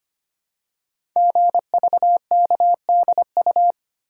Morse code, heard as a keyed tone, about 25 wpm, 690 Hz. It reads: GVKDU
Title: Code Groups - 5 Letter:Numbers - 25wpm